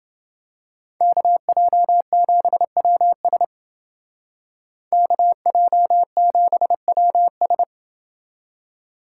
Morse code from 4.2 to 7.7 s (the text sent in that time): KJ7WS